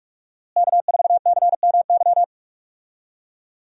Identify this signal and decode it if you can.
Morse code, heard as a keyed tone, about 45 wpm, 700 Hz. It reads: K4CMY
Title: USA Call Signs - 5 Characters - Repeated 2x - 45wpm